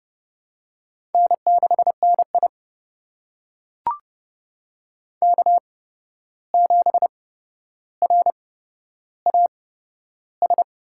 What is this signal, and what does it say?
Morse code, heard as a keyed tone, about 31 wpm, spaced as for 12 wpm, 700 Hz. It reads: N6NI K7RAS
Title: USA Call Signs - 8x Character Spacing - 30wpm